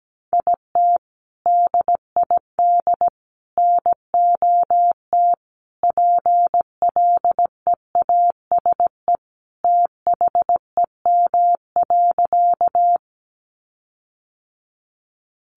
Morse code, heard as a keyed tone, about 17 wpm, 700 Hz. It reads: IT DID NOT PLEASE THEM.
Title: Sentences from Top 1500 Words 17wpm